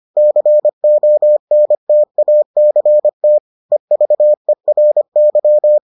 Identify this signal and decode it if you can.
Morse code, heard as a keyed tone, about 25 wpm, 600 Hz. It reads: CONTACT EVERY